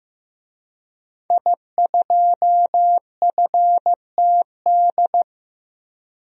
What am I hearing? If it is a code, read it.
Morse code, heard as a keyed tone, about 16 wpm, 700 Hz. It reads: I2FTD
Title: DXCC-100 Call Signs - 5 Characters - Repeated 3x - 15wpm